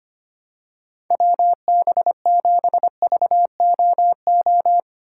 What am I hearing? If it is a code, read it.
Morse code, heard as a keyed tone, about 25 wpm, 700 Hz. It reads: WB7VOO
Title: USA Call Signs - 6 Characters - 3x Character Spacing - 25wpm